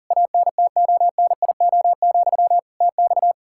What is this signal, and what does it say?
Morse code, heard as a keyed tone, about 40 wpm, 700 Hz. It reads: ANTONIO, TX
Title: Practice QSO - Exchange 1 - Beginning - 40wpm